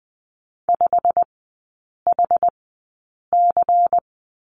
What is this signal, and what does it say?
Morse code, heard as a keyed tone, about 20 wpm, spaced as for 8 wpm, 700 Hz. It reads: 5HC
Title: 093 Lesson 16e - Letter c - Three Characters 20wpm